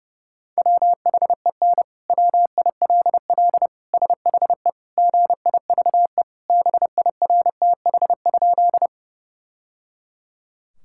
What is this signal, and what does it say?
Morse code, heard as a keyed tone, about 30 wpm, 700 Hz. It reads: WHEN WILL SHE GIVE BIRTH?
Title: Sentences from Top 1000 Words - Review Part 5 - 30wpm